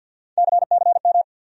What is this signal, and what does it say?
Morse code, heard as a keyed tone, about 51 wpm, 700 Hz. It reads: CCM
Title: USA Call Sign Postfixes - 3 Letters 50wpm